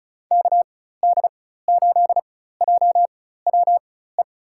Morse code, heard as a keyed tone, about 34 wpm, spaced as for 16 wpm, 700 Hz. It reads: KD8JWE
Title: USA Call Signs - 6 Characters - 4x Character Spacing - 35wpm